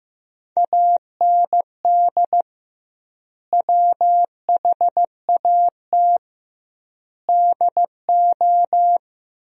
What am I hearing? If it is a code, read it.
Morse code, heard as a keyed tone, about 15 wpm, 700 Hz. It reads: AND WHAT DO
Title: 2x Word-Spacing - Sentences from Top 100 Words - 15wpm